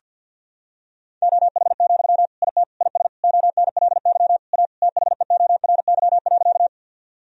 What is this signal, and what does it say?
Morse code, heard as a keyed tone, about 50 wpm, 700 Hz. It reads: OH, IT IS ONLY A THEORY.
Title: Sentences from Top 1000 Words - Review Part 3 - 50wpm